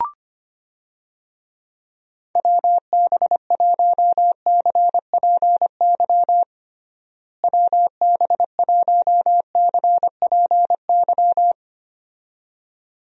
Morse code, heard as a keyed tone, about 25 wpm, 700 Hz. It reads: WB1CPY WB1CPY
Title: USA Call Signs - 6 Characters - Repeated 2x - 25wpm